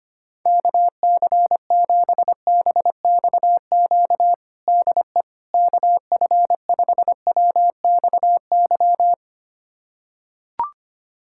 Morse code, heard as a keyed tone, about 25 wpm, 700 Hz. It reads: KC7BXQ DE KF5WXY
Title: USA Callsign DE Callsign 25wpm